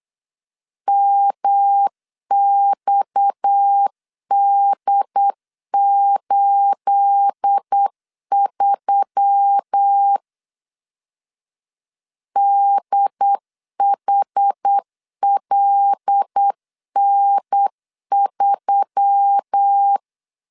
Morse code, heard as a keyed tone, about 8 wpm, 800 Hz. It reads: MXD83 DHLN3
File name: LEZIONI / 9 28 8wpm 3512